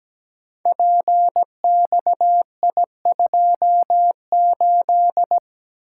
Morse code, heard as a keyed tone, about 17 wpm, 700 Hz. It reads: PXI28
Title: Code Groups - 5 Letter:Numbers - 17wpm